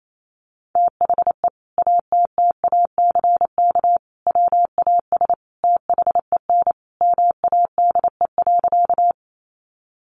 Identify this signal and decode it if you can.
Morse code, heard as a keyed tone, about 28 wpm, 700 Hz. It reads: THE ATTACK WAS THEN MADE.
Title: Sentences from Top 1500 Words 28wpm